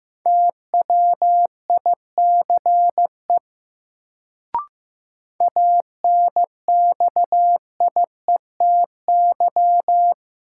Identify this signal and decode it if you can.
Morse code, heard as a keyed tone, about 15 wpm, 700 Hz. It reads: TWICE ANXIETY
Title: Sentences from Top 1600 Words - New Word Drill - 15wpm